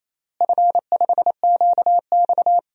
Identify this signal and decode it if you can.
Morse code, heard as a keyed tone, about 28 wpm, 700 Hz. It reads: F5QX
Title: DXCC-100 Call Signs - 4 Characters - Repeated 2x - 28wpm